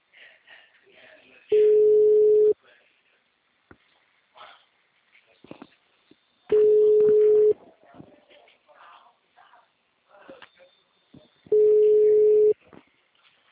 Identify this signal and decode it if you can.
Ringback tone (Europe, 425 Hz), with speech.